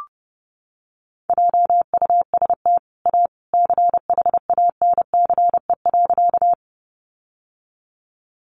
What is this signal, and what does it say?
Morse code, heard as a keyed tone, about 30 wpm, 700 Hz. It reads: JUST A CHANCE.
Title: Sentences from Top 800 Words 30wpm